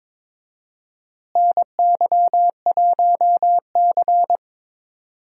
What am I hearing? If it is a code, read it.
Morse code, heard as a keyed tone, about 22 wpm, 700 Hz. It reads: NY1C
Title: USA Call Signs - 4 Characters - Repeated 2x - 22wpm